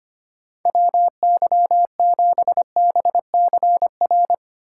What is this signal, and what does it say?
Morse code, heard as a keyed tone, about 31 wpm, 700 Hz. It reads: WY7BCR
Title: Speed Racing - USA 2x3 Call Signs 25wpm